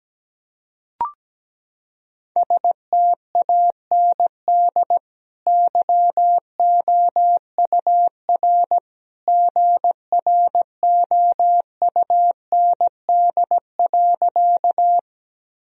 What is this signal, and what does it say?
Morse code, heard as a keyed tone, about 17 wpm, 700 Hz. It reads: STAND YOUR GROUND.
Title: Sentences from Top 1000 Words - Review Part 2 - 17wpm